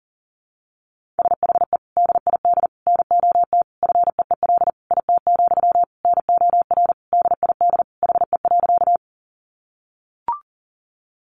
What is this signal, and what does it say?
Morse code, heard as a keyed tone, about 40 wpm, 700 Hz. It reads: SHE DID NOT FEEL IT, NOR DID HE.
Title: Sentences from Top 1500 Words 40wpm